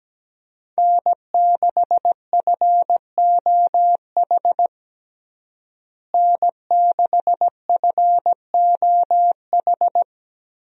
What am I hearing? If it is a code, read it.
Morse code, heard as a keyed tone, about 17 wpm, 700 Hz. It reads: N6FOH N6FOH
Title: USA Call Signs - 5 Characters - Repeated 3x - 17wpm